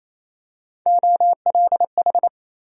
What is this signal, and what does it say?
Morse code, heard as a keyed tone, about 28 wpm, 700 Hz. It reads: OLH
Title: USA Call Sign Postfixes - 3 Letters 28wpm